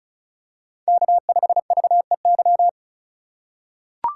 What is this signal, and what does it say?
Morse code, heard as a keyed tone, about 36 wpm, 700 Hz. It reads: K5VEY
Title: USA Call Signs - 5 Characters - 4x Character Spacing - 35wpm